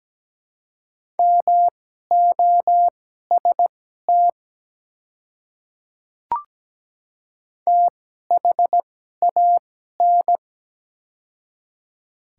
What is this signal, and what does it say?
Morse code, heard as a keyed tone, about 17 wpm, spaced as for 12 wpm, 700 Hz. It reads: MOST THAN
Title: 2x Character Spacing - Top 100 Words - 17wpm